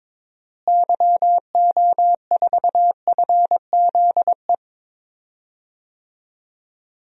Morse code, heard as a keyed tone, about 21 wpm, 700 Hz. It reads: YO4FZE
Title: DXCC-100 Call Signs - 6 Characters - Repeated 2x - 22wpm